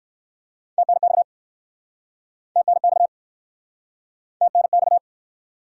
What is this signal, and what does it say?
Morse code, heard as a keyed tone, about 61 wpm, 700 Hz. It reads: TNX TNX TNX